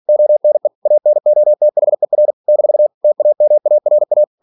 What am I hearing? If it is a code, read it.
Morse code, heard as a keyed tone, about 46 wpm, 600 Hz. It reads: ONE ANOTHER - TAMARA
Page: Never 47wpm